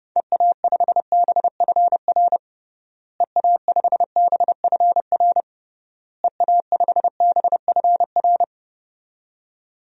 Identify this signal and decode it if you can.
Morse code, heard as a keyed tone, about 30 wpm, 700 Hz. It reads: EA5BFR EA5BFR EA5BFR
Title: DXCC-100 Call Signs - 6 Characters - Repeated 3x - 30wpm